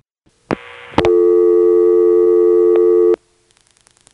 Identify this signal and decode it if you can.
Dial tone (North America), 350 + 440 Hz.